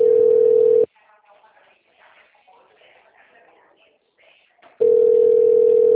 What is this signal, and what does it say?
Ringback tone (North America), 440 + 480 Hz, with speech.
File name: out- -2012- - - .231